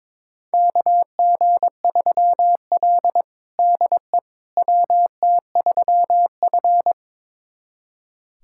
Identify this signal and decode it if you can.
Morse code, heard as a keyed tone, about 21 wpm, 700 Hz. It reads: KG3L DE WT3F
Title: USA Callsign DE Callsign - 4 Characters - 22wpm